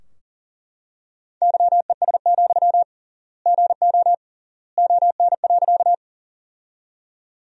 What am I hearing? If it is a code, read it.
Morse code, heard as a keyed tone, about 40 wpm, 700 Hz. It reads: YES, GO ON.
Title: 3x Word-Spacing - Sentences from Top 500 Words - 40wpm